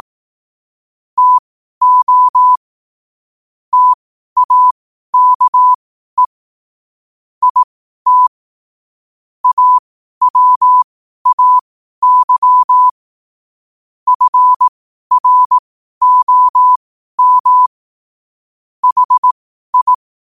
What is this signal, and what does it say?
Morse code, heard as a keyed tone, about 18 wpm, spaced as for 12 wpm, 1000 Hz. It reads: TO TAKE IT AWAY FROM HI
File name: qotdfiles / 12wpm